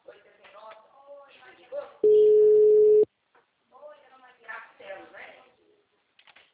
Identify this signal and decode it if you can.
Ringback tone (Europe, 425 Hz), with speech.